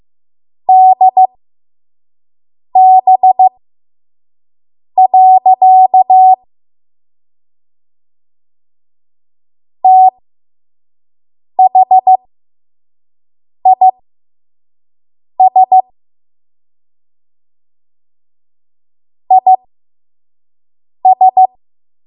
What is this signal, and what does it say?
Morse code, heard as a keyed tone, about 15 wpm, spaced as for 5 wpm, 750 Hz. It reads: DB. THIS IS
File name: Morse / 05WPM